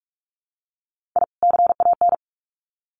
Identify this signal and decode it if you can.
Morse code, heard as a keyed tone, about 44 wpm, 700 Hz. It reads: I CAN
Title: Sets of 2 Words 45wpm